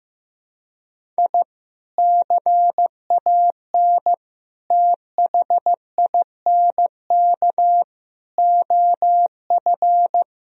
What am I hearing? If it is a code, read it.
Morse code, heard as a keyed tone, about 16 wpm, 700 Hz. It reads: I CAN THINK OF
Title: Sentences from Top 1000 Words - Review Part 1 - 15wpm